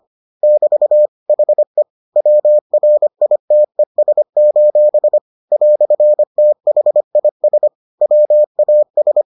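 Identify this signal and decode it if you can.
Morse code, heard as a keyed tone, about 25 wpm, 600 Hz. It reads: <BT> HE WRITES: "THIS WAS
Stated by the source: Morse edition of the Foundations of Amateur Radio episodes.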